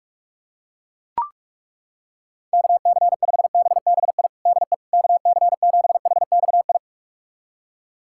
Morse code, heard as a keyed tone, about 41 wpm, 700 Hz. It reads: KC5BBI DE KC7HXI
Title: USA Callsign DE Callsign - 6 Characters - 45wpm